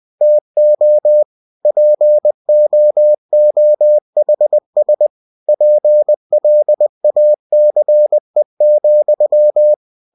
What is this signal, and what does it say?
Morse code, heard as a keyed tone, about 20 wpm, 600 Hz. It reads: TO POOHS PLACE,
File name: cw-books / winnie-the-pooh / winnie-20wpm-page-0084